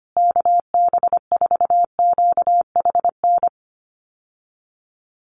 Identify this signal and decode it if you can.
Morse code, heard as a keyed tone, about 25 wpm, 700 Hz. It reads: KB4QHN
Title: USA Call Signs Repeated 3x 25wpm